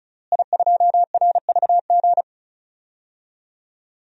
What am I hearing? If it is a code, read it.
Morse code, heard as a keyed tone, about 36 wpm, 700 Hz. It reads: I2RVG
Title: DXCC-100 Call Signs - 5 Characters - Repeated 3x - 35wpm